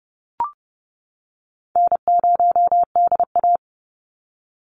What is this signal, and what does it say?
Morse code, heard as a keyed tone, about 30 wpm, 700 Hz. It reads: N0DA
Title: NA SOTA Chaser Callsigns 30wpm